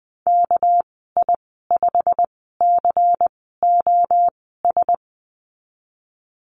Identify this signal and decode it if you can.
Morse code, heard as a keyed tone, about 20 wpm, spaced as for 15 wpm, 700 Hz. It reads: KI5COS
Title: 200 Lesson 32g - Letter k - Call Signs 20wpm